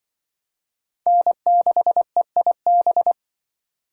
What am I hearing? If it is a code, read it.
Morse code, heard as a keyed tone, about 25 wpm, 700 Hz. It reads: N6EIB
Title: Speed Racing - USA 1x3 Call Signs 20wpm